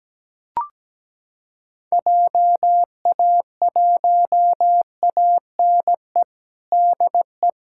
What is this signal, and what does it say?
Morse code, heard as a keyed tone, about 17 wpm, 700 Hz. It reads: JA1ANE DE
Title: USA Callsign DE Callsign 17wpm